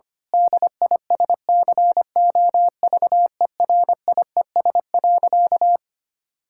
Morse code, heard as a keyed tone, about 25 wpm, 700 Hz. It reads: DISCOVERIES.
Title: Pangram Challenge v2 - 25wpm